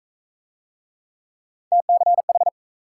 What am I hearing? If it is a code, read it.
Morse code, heard as a keyed tone, about 40 wpm, 700 Hz. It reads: TCH